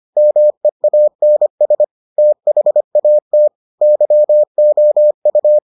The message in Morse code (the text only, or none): MEANS THAT YOU